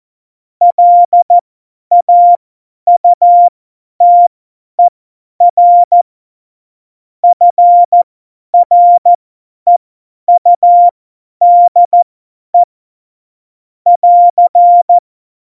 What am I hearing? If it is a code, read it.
Morse code, heard as a keyed tone, about 14 wpm, spaced as for 10 wpm, 700 Hz. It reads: LAUTER FREUDE <AR>
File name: Morsekurs / Witze / Witz 08. Oktober 2019, 50 F70